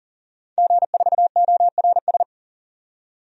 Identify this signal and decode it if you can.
Morse code, heard as a keyed tone, about 39 wpm, 700 Hz. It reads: G4ORS
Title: DXCC-100 Call Signs - 5 Characters - 40wpm